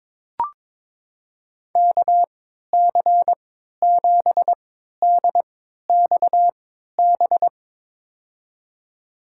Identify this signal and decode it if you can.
Morse code, heard as a keyed tone, about 22 wpm, spaced as for 13 wpm, 700 Hz. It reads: KC7DXB
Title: USA Call Signs - 3x Character Spacing - 22wpm